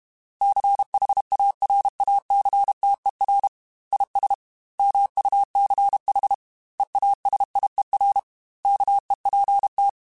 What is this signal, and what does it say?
Morse code, heard as a keyed tone, about 32 wpm, 800 Hz. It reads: CHARACTER IS MUCH EASIER KEPT
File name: Quotes / Quotes-mp3s / Character @30